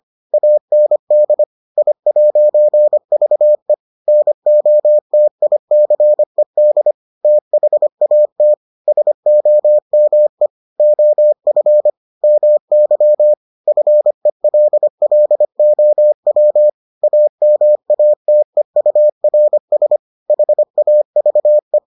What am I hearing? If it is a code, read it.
Morse code, heard as a keyed tone, about 25 wpm, 600 Hz. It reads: AND I'VE NOTICED THAT SOME OF MY FELLOW AMATEURS HAVE